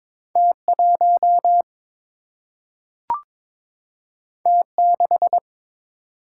Morse code, heard as a keyed tone, about 22 wpm, 700 Hz. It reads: T1 T6
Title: Single-Letter Single-Number 22wpm